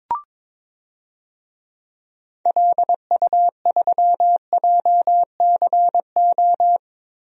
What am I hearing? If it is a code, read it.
Morse code, heard as a keyed tone, about 22 wpm, 700 Hz. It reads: LU3JCO
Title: DXCC-100 Call Signs - 6 Characters - Repeated 3x - 22wpm